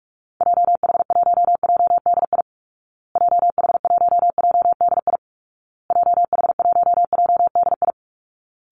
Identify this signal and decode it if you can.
Morse code, heard as a keyed tone, about 44 wpm, 700 Hz. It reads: JH1JDI JH1JDI JH1JDI
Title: Call Signs Repeated 2x - 45wpm